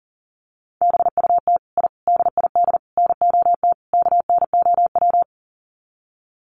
Morse code, heard as a keyed tone, about 40 wpm, 700 Hz. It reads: BUT I DID NOT KNOW
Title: Sets of 5 Words Encore 40wpm